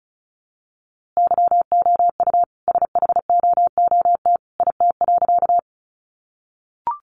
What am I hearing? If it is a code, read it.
Morse code, heard as a keyed tone, about 34 wpm, 700 Hz. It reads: YOU SHOOT IT.
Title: A Sound Of Thunder 35wpm